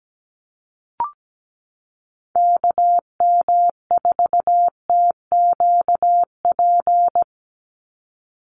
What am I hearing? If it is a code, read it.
Morse code, heard as a keyed tone, about 17 wpm, 700 Hz. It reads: KM4TQP
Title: USA Call Signs 17wpm